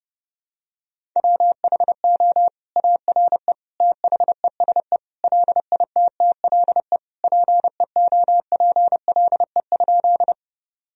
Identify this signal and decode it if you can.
Morse code, heard as a keyed tone, about 31 wpm, 700 Hz. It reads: WHO ARE THESE LITTLE PEOPLE?
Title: Sentences from Top 1000 Words - Review Part 1 - 30wpm